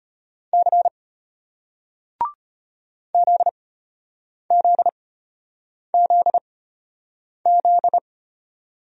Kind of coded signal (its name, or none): Morse code